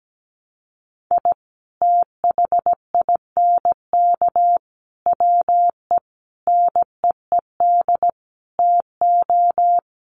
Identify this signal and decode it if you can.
Morse code, heard as a keyed tone, about 17 wpm, 700 Hz. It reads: I THINK WE NEED TO
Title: Sets of 5 Words 17wpm